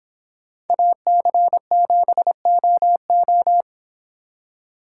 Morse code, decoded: AC7OO

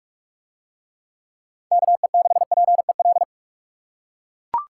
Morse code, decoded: KE6PEL

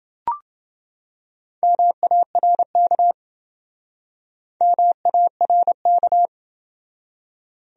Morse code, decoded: MARK MARK

